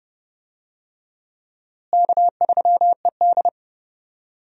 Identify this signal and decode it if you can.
Morse code, heard as a keyed tone, about 30 wpm, 700 Hz. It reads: K3ED
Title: USA Call Signs - 4 Characters - Repeated 3x - 30wpm